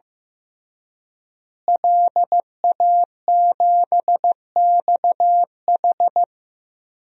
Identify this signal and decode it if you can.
Morse code, heard as a keyed tone, about 15 wpm, 700 Hz. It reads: LA7XH
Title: DXCC-100 Call Signs - 5 Characters - Repeated 3x - 15wpm